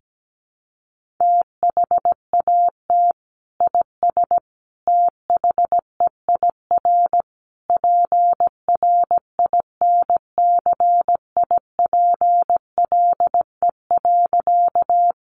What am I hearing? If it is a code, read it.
Morse code, heard as a keyed tone, about 17 wpm, 700 Hz. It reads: THAT IS THEIR PRINCIPLE.
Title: Sentences from Top 1400 Words 17wpm